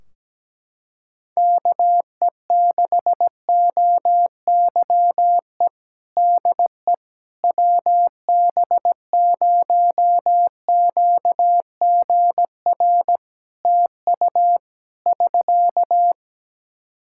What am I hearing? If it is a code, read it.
Morse code, heard as a keyed tone, about 17 wpm, 700 Hz. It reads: KE6OYE DE WB0QGR TU <SK>
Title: Practice QSO - Exchange 4 - Ending - 17wpm